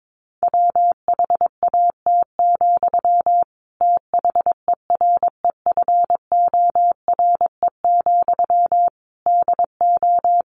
Morse code, heard as a keyed tone, about 22 wpm, 700 Hz. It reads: WHAT, THEREFORE, DO
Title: Sentences from Top 1100 Words 22wpm